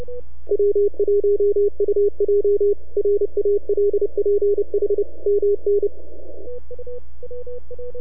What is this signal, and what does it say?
Morse code, heard as a keyed tone, about 29 wpm, 410 Hz. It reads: W1UJ RALPH MN
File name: NAQPCW-8-2009 / W1UJ NAQP CW 09 1930Z